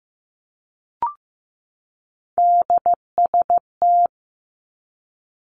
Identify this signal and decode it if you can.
Morse code, heard as a keyed tone, about 15 wpm, 700 Hz. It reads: DST
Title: Three Challenging Letter-Number Combinations 15wpm